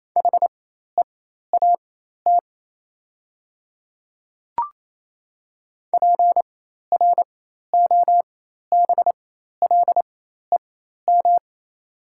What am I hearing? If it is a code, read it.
Morse code, heard as a keyed tone, about 28 wpm, spaced as for 13 wpm, 700 Hz. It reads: HEAT PROBLEM
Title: 4x Character Spacing - Top 500 Words - 28wpm